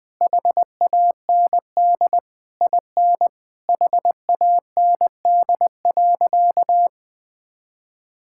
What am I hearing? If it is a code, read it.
Morse code, heard as a keyed tone, about 21 wpm, 700 Hz. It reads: HAND IN HAND.
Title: Sentences from Top 1000 Words - Review Part 5 - 20wpm